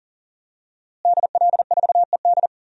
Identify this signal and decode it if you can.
Morse code, heard as a keyed tone, about 40 wpm, 700 Hz. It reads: DL4ED
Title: DXCC-100 Call Signs - 5 Characters - Repeated 2x - 40wpm